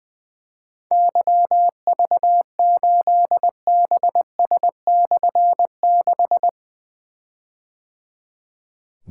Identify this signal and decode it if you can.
Morse code, heard as a keyed tone, about 20 wpm, 700 Hz. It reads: YV8BS/6
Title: Difficult Call Signs 20wpm